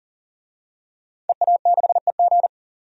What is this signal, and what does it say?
Morse code, heard as a keyed tone, about 40 wpm, 700 Hz. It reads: EA6EG
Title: DXCC-100 Call Signs - 5 Characters - 40wpm